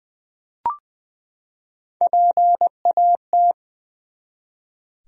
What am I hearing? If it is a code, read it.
Morse code, heard as a keyed tone, about 20 wpm, 700 Hz. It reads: PAT
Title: USA Common Names - Max 3 Letters - 20wpm